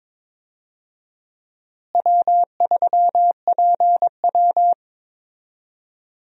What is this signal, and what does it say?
Morse code, heard as a keyed tone, about 22 wpm, 700 Hz. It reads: W3PW
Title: USA Call Signs - 4 Characters - Repeated 3x - 22wpm